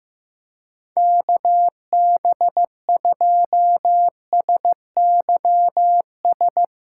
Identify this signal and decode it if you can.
Morse code, heard as a keyed tone, about 15 wpm, 700 Hz. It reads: KB2SYS
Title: USA Callsign DE Callsign - 6 Characters - 15wpm